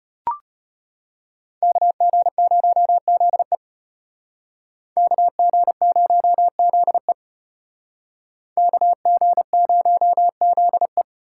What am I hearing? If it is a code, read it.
Morse code, heard as a keyed tone, about 30 wpm, 700 Hz. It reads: KG0ZE KG0ZE KG0ZE
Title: Speed Racing - USA 2x2 Call Signs 25wpm